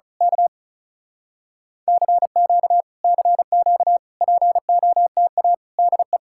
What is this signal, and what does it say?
Morse code, heard as a keyed tone, about 33 wpm, 700 Hz. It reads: K CQ CQ POTA DE
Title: Speed-Racing Common QSO Phrases - 35wpm